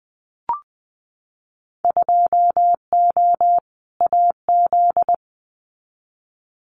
Morse code, heard as a keyed tone, about 20 wpm, 700 Hz. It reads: 2O AZ
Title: Winter Field Day Exchanges 20wpm